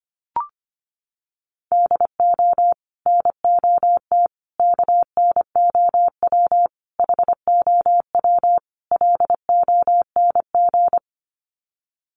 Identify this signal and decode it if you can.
Morse code, heard as a keyed tone, about 25 wpm, 700 Hz. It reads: DO NOT KNOW HOW LONG
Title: Sets of 5 Words Encore2 25wpm